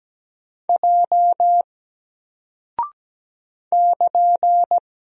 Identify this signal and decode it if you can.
Morse code, heard as a keyed tone, about 17 wpm, 700 Hz. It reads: J <KN>
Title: Single Character (Letter:Number:Punctuation:Prosign) Alt 17wpm